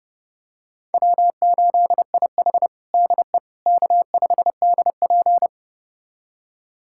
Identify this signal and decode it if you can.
Morse code, heard as a keyed tone, about 32 wpm, 700 Hz. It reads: W8IH DE K5DP
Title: USA Callsign DE Callsign - 4 Characters - 30wpm